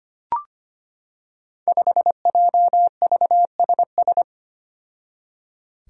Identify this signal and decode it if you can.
Morse code, heard as a keyed tone, about 25 wpm, 700 Hz. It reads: 5JVSS
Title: Five Challenging Letter-Number Combinations 25wpm